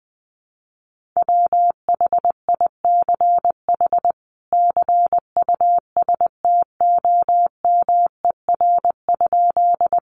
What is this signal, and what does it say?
Morse code, heard as a keyed tone, about 20 wpm, 700 Hz. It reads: WHICH CUSTOMER?
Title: Sentences from Top 1100 Words 20wpm